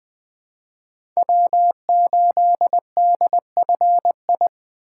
Morse code, heard as a keyed tone, about 20 wpm, 700 Hz. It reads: W8DFI
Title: USA Call Signs - 5 Characters - 4x Character Spacing - 20wpm